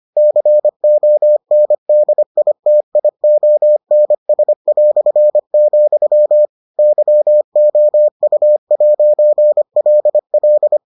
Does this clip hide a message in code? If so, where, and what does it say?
Yes: Morse code from 0.2 to 10.8 s, heard as a keyed tone, about 25 wpm, 600 Hz. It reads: CONDITIONS", YOU'LL